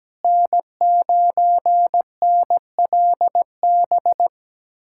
Morse code, heard as a keyed tone, about 17 wpm, 700 Hz. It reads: N9NLB
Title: USA Callsign DE Callsign 17wpm